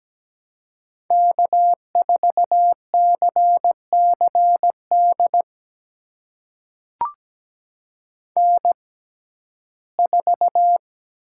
Morse code, heard as a keyed tone, about 17 wpm, spaced as for 12 wpm, 700 Hz. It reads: K4CCD N4
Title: USA Call Signs - 5 Characters - 6x Character Spacing - 17wpm